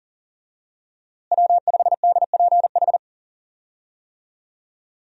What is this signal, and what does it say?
Morse code, heard as a keyed tone, about 40 wpm, 700 Hz. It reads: W5DPH
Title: USA Call Signs - 5 Characters - Repeated 2x - 40wpm